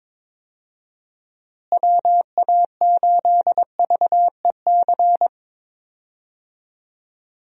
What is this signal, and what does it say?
Morse code, heard as a keyed tone, about 22 wpm, 700 Hz. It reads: WA8VEC
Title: USA Call Signs - 6 Characters - Repeated 2x - 22wpm